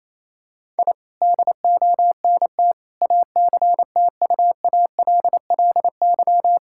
Morse code, heard as a keyed tone, about 28 wpm, 700 Hz. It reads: I DONT ACTUALLY
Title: Sentences from Top 1000 Words - Review Part 5 - 28wpm